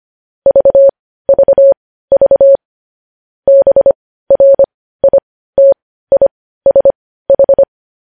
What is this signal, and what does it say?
Morse code, heard as a keyed tone, about 25 wpm, spaced as for 15 wpm, 550 Hz. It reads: VVV BRITISH
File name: podcast / News headlines-1x-25x15